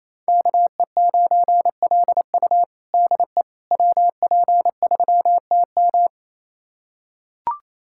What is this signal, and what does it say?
Morse code, heard as a keyed tone, about 28 wpm, 700 Hz. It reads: KE9LU DE WP3TM
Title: USA Callsign DE Callsign - 5 Characters - 28wpm